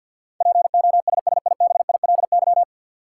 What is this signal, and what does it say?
Morse code, heard as a keyed tone, about 52 wpm, 700 Hz. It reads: POSSIBILY
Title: Sentences from Top 1300 Words - New Word Drill - 50wpm